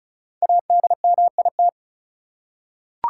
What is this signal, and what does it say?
Morse code, heard as a keyed tone, about 36 wpm, 700 Hz. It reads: ADMIT
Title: Five Letter Words - 35wpm